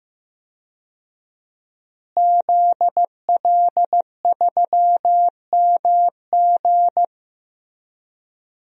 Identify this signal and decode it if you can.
Morse code, heard as a keyed tone, about 15 wpm, 700 Hz. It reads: ZL3MG
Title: DXCC-100 Call Signs - 5 Characters - Repeated 2x - 15wpm